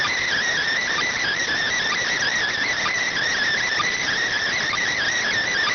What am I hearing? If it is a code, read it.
Emergency Alert System (SAME) tones (header bursts), heard as harsh buzzing bursts. It is heard from 0.7 to 5.1 s.